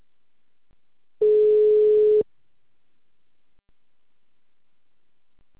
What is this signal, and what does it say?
Ringback tone (Europe, 425 Hz).
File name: out- -2044- - - .11434